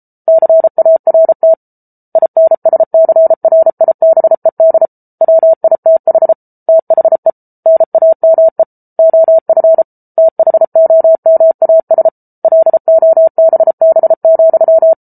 Morse code, heard as a keyed tone, about 34 wpm, 650 Hz. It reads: CART INSCRIBED WITH THE NAME OF THOMAS LOBB,